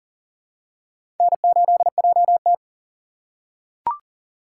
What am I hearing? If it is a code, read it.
Morse code, heard as a keyed tone, about 40 wpm, 700 Hz. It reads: N8JT
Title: USA Call Signs - 4 Characters - 4x Character Spacing - 40wpm